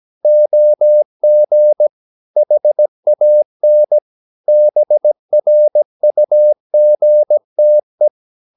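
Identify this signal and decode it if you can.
Morse code, heard as a keyed tone, about 17 wpm, 600 Hz. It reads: OG HAN BRUGTE
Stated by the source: Slow (QRS) Morse practice at 17 wpm, from CW med Gnister.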